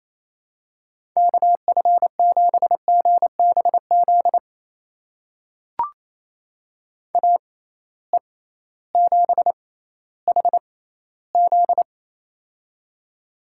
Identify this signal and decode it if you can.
Morse code, heard as a keyed tone, about 28 wpm, spaced as for 15 wpm, 700 Hz. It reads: KF7GBZ AE7HZ
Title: USA Call Signs - 6x Character Spacing - 28wpm